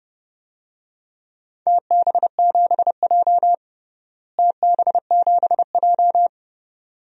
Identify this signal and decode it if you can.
Morse code, heard as a keyed tone, about 30 wpm, 700 Hz. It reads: TB7J TB7J
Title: DXCC-100 Call Signs - 4 Characters - Repeated 2x - 30wpm